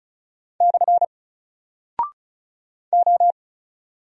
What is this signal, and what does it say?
Morse code, heard as a keyed tone, about 36 wpm, 700 Hz. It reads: / O
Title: Single Character (Letter:Number:Punctuation:Prosign) 35wpm